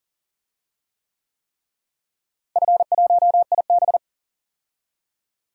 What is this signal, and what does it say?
Morse code, heard as a keyed tone, about 40 wpm, 700 Hz. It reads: F1IB
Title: DXCC-100 Call Signs - 4 Characters - Repeated 3x - 40wpm